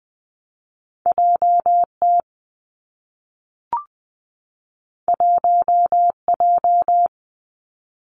Morse code, heard as a keyed tone, about 20 wpm, 700 Hz. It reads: J T 1 J
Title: Two Challenging Letter-Number Combinations 20wpm